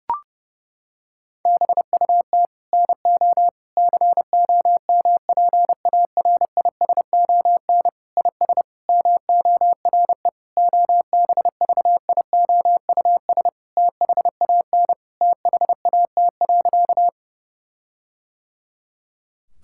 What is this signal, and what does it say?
Morse code, heard as a keyed tone, about 30 wpm, 700 Hz. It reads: BUT NO COMPARISON IS MORE OBVIOUS THAN THAT.
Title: Sentences from Top 1600 Words 30wpm